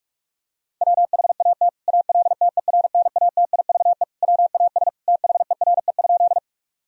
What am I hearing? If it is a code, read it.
Morse code, heard as a keyed tone, about 48 wpm, 700 Hz. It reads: WHAT ALTERNATIVE WAS THERE?
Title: Sentences from Top 1000 Words - Review Part 4 - 45wpm